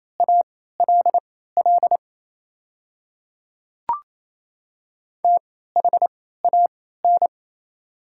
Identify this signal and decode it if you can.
Morse code, heard as a keyed tone, about 28 wpm, spaced as for 16 wpm, 700 Hz. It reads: ALL THAN
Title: 3x Character Spacing - Top 100 Words - 28wpm